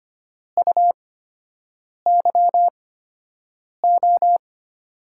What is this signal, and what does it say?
Morse code, heard as a keyed tone, about 25 wpm, spaced as for 7 wpm, 700 Hz. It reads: UYO
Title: USA Call Signs - 8x Character Spacing - 25wpm